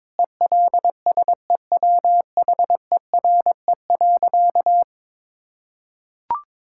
Morse code, heard as a keyed tone, about 21 wpm, 700 Hz. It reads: ELSEWHERE.
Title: Sentences from Top 1600 Words 22wpm